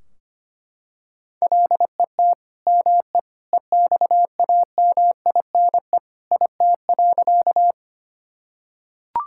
Morse code, heard as a keyed tone, about 25 wpm, 700 Hz. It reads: LET ME EXAMINE IT.
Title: Sentences from Top 1600 Words 25wpm